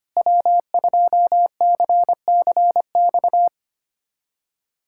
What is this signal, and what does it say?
Morse code, heard as a keyed tone, about 25 wpm, 700 Hz. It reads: W2CCX
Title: USA Callsign DE Callsign - 5 Characters - 25wpm